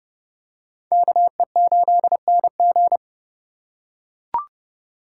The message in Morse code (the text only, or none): KE8NG